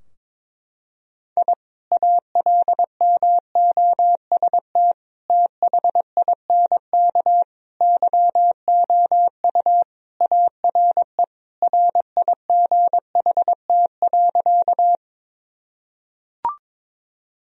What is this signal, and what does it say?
Morse code, heard as a keyed tone, about 22 wpm, 700 Hz. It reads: I ALMOST THINK YOU ARE RIGHT.
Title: Sentences from Top 1000 Words - Review Part 4 - 22wpm